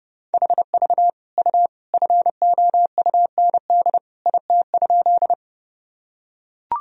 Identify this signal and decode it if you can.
Morse code, heard as a keyed tone, about 32 wpm, 700 Hz. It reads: HV U FOUND IT?
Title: Sentences from Top 300 Words and Abbreviations 30wpm